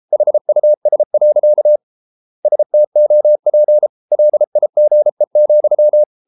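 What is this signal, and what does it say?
Morse code, heard as a keyed tone, about 33 wpm, 600 Hz. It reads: HUS. STOP LIGE,
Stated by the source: Morse code at 33 wpm, QRQ.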